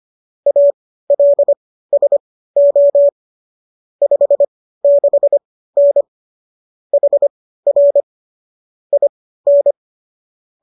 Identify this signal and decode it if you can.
Morse code, heard as a keyed tone, about 25 wpm, spaced as for 15 wpm, 570 Hz. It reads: ALSO 56N HR IN
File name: qso201 15